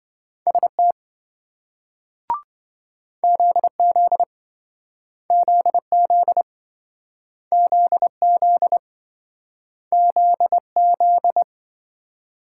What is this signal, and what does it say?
Morse code, heard as a keyed tone, about 26 wpm, 700 Hz. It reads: ST ZZ ZZ ZZ ZZ